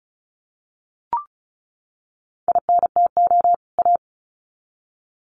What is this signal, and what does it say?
Morse code, heard as a keyed tone, about 34 wpm, 700 Hz. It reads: INTO A
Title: Sets of 2 Words 35wpm